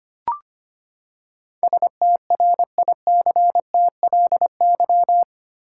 Morse code, heard as a keyed tone, about 25 wpm, 700 Hz. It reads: STRICTLY
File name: Eight Letter Words - 25wpm